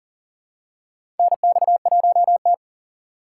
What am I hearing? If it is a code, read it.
Morse code, heard as a keyed tone, about 40 wpm, 700 Hz. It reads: NX1T
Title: USA Call Signs - 4 Characters - 4x Character Spacing - 40wpm